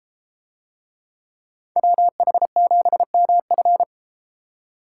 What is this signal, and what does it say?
Morse code, heard as a keyed tone, about 31 wpm, 700 Hz. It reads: WH7MF